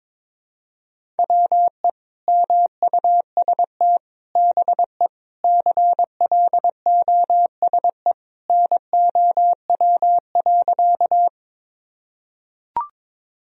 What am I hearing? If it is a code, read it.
Morse code, heard as a keyed tone, about 22 wpm, 700 Hz. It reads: WE MUST BE CLOSE NOW.
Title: Sentences from Top 1000 Words - Review Part 1 - 22wpm